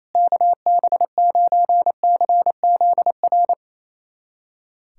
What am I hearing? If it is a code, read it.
Morse code, heard as a keyed tone, about 28 wpm, 700 Hz. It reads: KB9CZR
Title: USA Callsign DE Callsign - 6 Characters - 28wpm